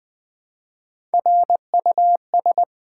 Morse code, heard as a keyed tone, about 20 wpm, 700 Hz. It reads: RUS
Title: Build A Word - 1 Letter At A Time - 3 to 5-Letter Words - 20wpm